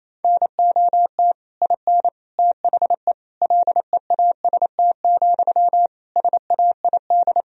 Morse code, heard as a keyed tone, about 28 wpm, 700 Hz. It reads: NOT IN THE LEAST, SAID
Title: Sentences from Top 1000 Words - Review Part 4 - 28wpm